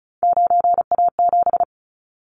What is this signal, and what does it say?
Morse code, heard as a keyed tone, about 36 wpm, 700 Hz. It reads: 9A7
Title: Callsign Prefixes for Top 25 DXCC - 35wpm